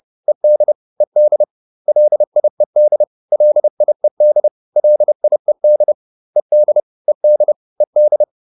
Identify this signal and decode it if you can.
Morse code, heard as a keyed tone, about 30 wpm, 600 Hz. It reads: ED ED LIED LIED LIED ED ED ED
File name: ED4 30